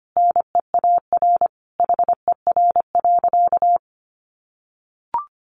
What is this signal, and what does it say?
Morse code, heard as a keyed tone, about 25 wpm, 700 Hz. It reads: NEAR HER.
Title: Easy Phrases 25wpm